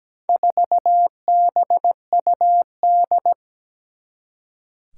Morse code, heard as a keyed tone, about 17 wpm, 700 Hz. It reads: 4BUD
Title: Four Challenging Letter-Number Combinations 17wpm